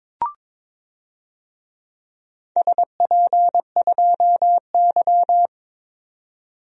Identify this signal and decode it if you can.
Morse code, heard as a keyed tone, about 22 wpm, 700 Hz. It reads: SP2Y
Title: DXCC-100 Call Signs - 4 Characters - 22wpm